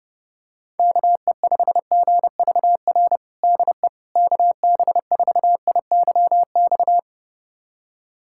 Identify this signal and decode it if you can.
Morse code, heard as a keyed tone, about 32 wpm, 700 Hz. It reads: KE5GVR DE KB4IYX
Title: USA Callsign DE Callsign 30wpm